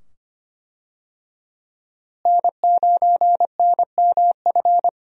Morse code, heard as a keyed tone, about 25 wpm, 700 Hz. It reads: N9NMF
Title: USA Call Signs - 5 Characters - Repeated 2x - 25wpm